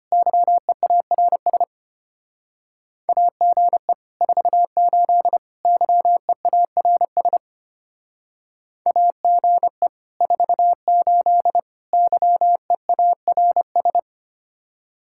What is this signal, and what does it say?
Morse code, heard as a keyed tone, about 30 wpm, 700 Hz. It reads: YEARS AGE 48 YEARS AGE 48 YEARS
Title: Speed-Racing Common QSO Phrases Encore - 25wpm